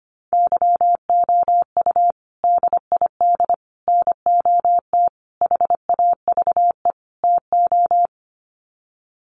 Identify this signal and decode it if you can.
Morse code, heard as a keyed tone, about 25 wpm, 700 Hz. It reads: YOU DID NOT HAVE TO
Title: Sets of 5 Words Encore 25wpm